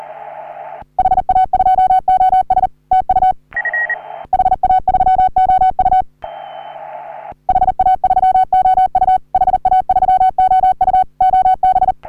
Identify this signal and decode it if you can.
Morse code, heard as a keyed tone, about 40 wpm, 750 Hz. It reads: HA2OS TU HA3OU HA3OU HA3OU OB
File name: CW / HA3OU RHOBCW REC